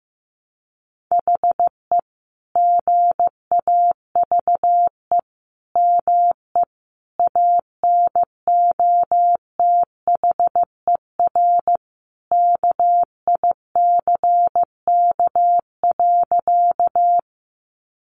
Morse code, heard as a keyed tone, about 15 wpm, 700 Hz. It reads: HE GAVE ME ANOTHER KICK.
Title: Sentences from Top 1300 Words 15wpm